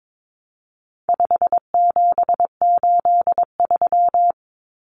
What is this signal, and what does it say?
Morse code, heard as a keyed tone, about 22 wpm, 700 Hz. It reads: 5783
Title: Numbers - 4-Digit 22wpm